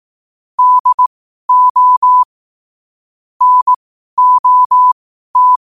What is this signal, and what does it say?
Morse code, heard as a keyed tone, about 18 wpm, spaced as for 12 wpm, 1000 Hz. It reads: DO NOT
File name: qotdfiles / 12wpm